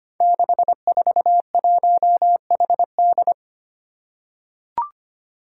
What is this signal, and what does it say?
Morse code, heard as a keyed tone, about 25 wpm, 700 Hz. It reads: 641HD
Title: Five Challenging Letter-Number Combinations 25wpm